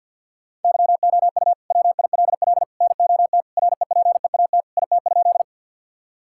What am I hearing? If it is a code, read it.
Morse code, heard as a keyed tone, about 50 wpm, 700 Hz. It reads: YOU WILL NOT REPEAT IT?
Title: Sentences from Top 1600 Words 50wpm